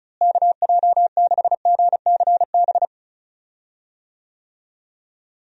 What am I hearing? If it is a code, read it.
Morse code, heard as a keyed tone, about 36 wpm, 700 Hz. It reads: KJ6GCB
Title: USA Call Signs - 6 Characters - Repeated 2x - 35wpm